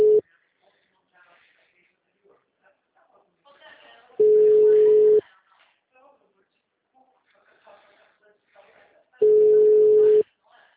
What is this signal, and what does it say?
Ringback tone (Europe, 425 Hz), with speech.